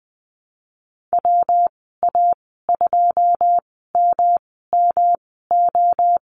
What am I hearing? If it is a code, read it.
Morse code, heard as a keyed tone, about 20 wpm, spaced as for 15 wpm, 700 Hz. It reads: WA2MMO
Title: 173 Lesson 28g - Number 9 - Call Signs 20wpm